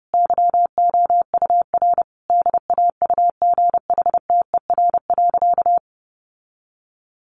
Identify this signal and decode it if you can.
Morse code, heard as a keyed tone, about 30 wpm, 700 Hz. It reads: YOUR DAUGHTER.
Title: Sentences from Top 1500 Words 30wpm